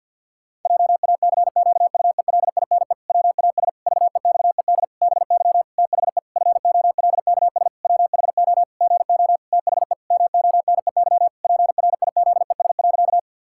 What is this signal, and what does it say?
Morse code, heard as a keyed tone, about 52 wpm, 700 Hz. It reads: JACQUELINE WAS VEXED BY THE FOLKS WHO GO THE MONEY PRIZES.
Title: Pangram Challenge v2 - 50wpm